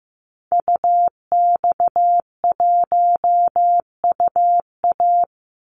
Morse code, heard as a keyed tone, about 15 wpm, 700 Hz. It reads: UX1UA
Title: Call Signs Repeated 3x - 15wpm